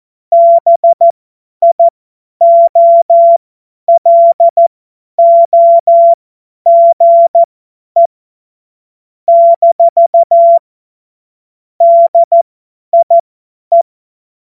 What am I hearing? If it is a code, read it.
Morse code, heard as a keyed tone, about 14 wpm, spaced as for 10 wpm, 680 Hz. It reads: BIOLOGE - DIE
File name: Morsekurs / Witze / Witz 26. April 2022 50 F70